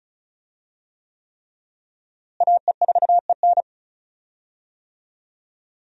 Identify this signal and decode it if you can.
Morse code, heard as a keyed tone, about 36 wpm, 700 Hz. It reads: AE4EN
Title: USA Call Signs - 5 Characters - Repeated 3x - 35wpm